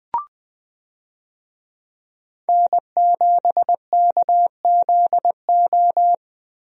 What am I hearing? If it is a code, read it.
Morse code, heard as a keyed tone, about 20 wpm, 700 Hz. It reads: N7KZO
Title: USA Call Signs - 5 Characters - Repeated 3x - 20wpm